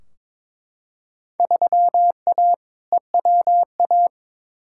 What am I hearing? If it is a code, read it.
Morse code, heard as a keyed tone, about 22 wpm, 700 Hz. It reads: 3A EWA
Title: Field Day Exchanges Encore - 22wpm